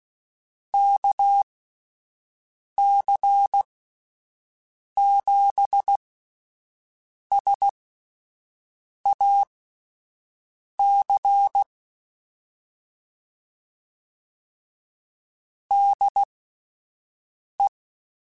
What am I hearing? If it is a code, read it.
Morse code, heard as a keyed tone, about 16 wpm, spaced as for 5 wpm, 800 Hz. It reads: KC7SAC DE
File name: QSOs Spaced